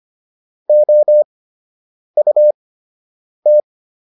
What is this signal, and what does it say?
Morse code, heard as a keyed tone, about 25 wpm, spaced as for 8 wpm, 600 Hz. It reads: OUT
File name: WD104 08